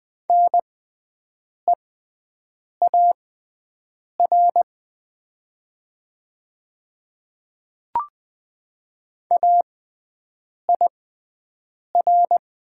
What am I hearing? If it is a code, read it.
Morse code, heard as a keyed tone, about 20 wpm, spaced as for 7 wpm, 700 Hz. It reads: NEAR AIR
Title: 6x Character Spacing - Top 300 Words - 20wpm